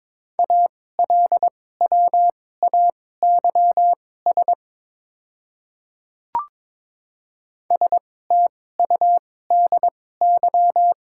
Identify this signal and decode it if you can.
Morse code, heard as a keyed tone, about 22 wpm, spaced as for 16 wpm, 690 Hz. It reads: ALWAYS STUDY
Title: 2x Character Spacing - Top 300 Words - 22wpm